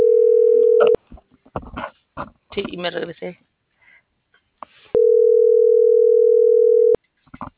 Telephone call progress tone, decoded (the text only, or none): ringback tone